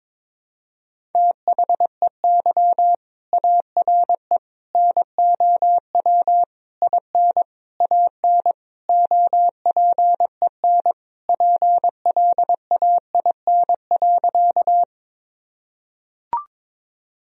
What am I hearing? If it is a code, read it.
Morse code, heard as a keyed tone, about 22 wpm, 700 Hz. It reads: THEY ARE NOW IN AN OPEN PLAIN.
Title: Sentences from Top 1000 Words - Review Part 2 - 22wpm